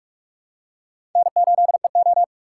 Morse code, heard as a keyed tone, about 46 wpm, 700 Hz. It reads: N8EO
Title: USA Call Signs - 2x Character Spacing - 45wpm